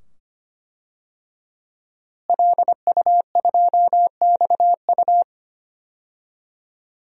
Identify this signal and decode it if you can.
Morse code, heard as a keyed tone, about 25 wpm, 700 Hz. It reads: LU2XU
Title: DXCC-100 Call Signs - 5 Characters - Repeated 2x - 25wpm